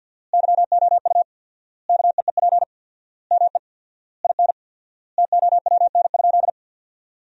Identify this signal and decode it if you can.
Morse code, heard as a keyed tone, about 53 wpm, 700 Hz. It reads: YOU KEEP ME IN TOWN?
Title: 4x Word-Spacing - Sentences from Top 500 Words - 50wpm